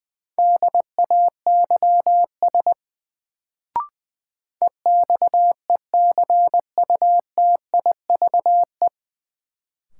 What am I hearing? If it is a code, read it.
Morse code, heard as a keyed tone, about 20 wpm, 700 Hz. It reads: DAYS EXECUTIVE
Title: Sets of 5 Words Encore - New Word Drill - 20wpm